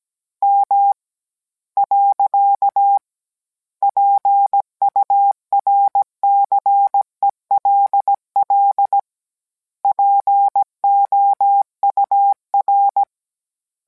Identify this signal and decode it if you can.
Morse code, heard as a keyed tone, about 17 wpm, 800 Hz. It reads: M . PURCELL POUR